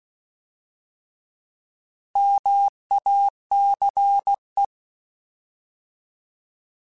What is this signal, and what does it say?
Morse code, heard as a keyed tone, about 16 wpm, 800 Hz. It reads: MACE
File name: Alphabetical / Alphabetical mp3s / Ms @15